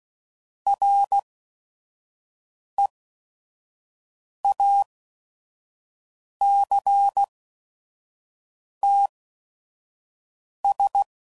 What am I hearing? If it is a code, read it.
Morse code, heard as a keyed tone, about 16 wpm, spaced as for 5 wpm, 800 Hz. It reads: REACTS
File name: CW Code Groups @5WPM / Technical Subjects / XC